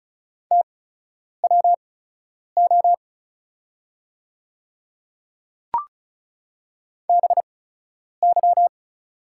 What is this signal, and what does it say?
Morse code, heard as a keyed tone, about 36 wpm, spaced as for 10 wpm, 700 Hz. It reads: TWO BY
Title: 8x Character Spacing - Top 100 Words - 35wpm